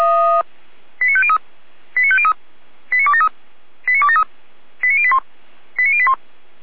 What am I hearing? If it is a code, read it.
Five-tone selective call (ZVEI), heard as quick runs of five tones, tones of about 70 ms. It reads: 89503 89503 89273 89273 89081 89081
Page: /455/SDR/ZVEI/89150/